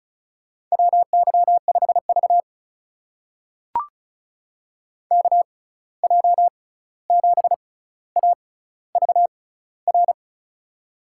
Morse code, heard as a keyed tone, about 36 wpm, spaced as for 16 wpm, 700 Hz. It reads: WY5V KJ7AVR
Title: USA Call Signs - 6x Character Spacing - 35wpm